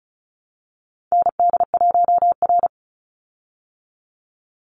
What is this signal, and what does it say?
Morse code, heard as a keyed tone, about 34 wpm, 700 Hz. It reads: ND1R
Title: Call Signs Repeated 2x - 35wpm